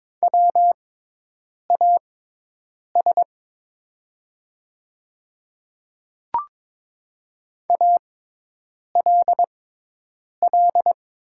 Morse code, heard as a keyed tone, about 22 wpm, spaced as for 8 wpm, 700 Hz. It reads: WAS ALL
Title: 6x Character Spacing - Top 100 Words - 22wpm